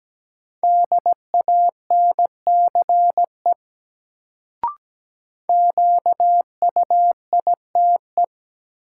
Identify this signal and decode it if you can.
Morse code, heard as a keyed tone, about 17 wpm, 700 Hz. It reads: DANCE QUITE
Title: Sentences from Top 700 Words - New Word Drill - 17wpm